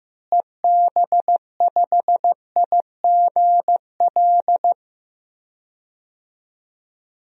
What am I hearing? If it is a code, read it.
Morse code, heard as a keyed tone, about 15 wpm, 700 Hz. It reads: EB5IGL
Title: DXCC-100 Call Signs - 6 Characters - Repeated 3x - 15wpm